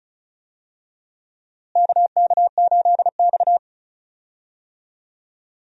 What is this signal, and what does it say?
Morse code, heard as a keyed tone, about 36 wpm, 700 Hz. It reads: KK8X
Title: USA Call Signs - 4 Characters - Repeated 2x - 35wpm